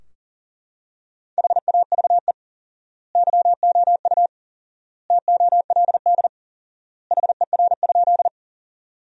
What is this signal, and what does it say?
Morse code, heard as a keyed tone, about 40 wpm, 700 Hz. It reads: HAVE YOU TOLD HER?
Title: 4x Word-Spacing - Sentences from Top 400 Words - 40wpm